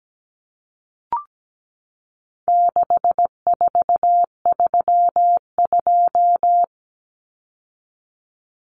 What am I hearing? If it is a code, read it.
Morse code, heard as a keyed tone, about 17 wpm, 700 Hz. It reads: 6432
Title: Common Cut Numbers - 4-Digit 17wpm